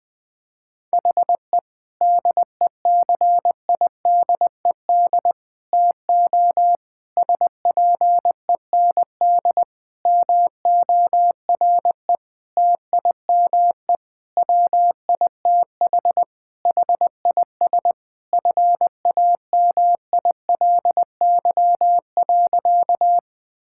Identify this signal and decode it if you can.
Morse code, heard as a keyed tone, about 20 wpm, 700 Hz. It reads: HE DECIDED TO SPEND MORE TIME WITH HIS FAMILY.
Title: Sentences with 7-Word Phrases - Spoken Beginning - 20wpm